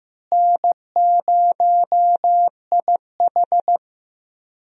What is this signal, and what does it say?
Morse code, heard as a keyed tone, about 15 wpm, 700 Hz. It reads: N0IH
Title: USA Call Signs - 4 Characters - Repeated 2x - 15wpm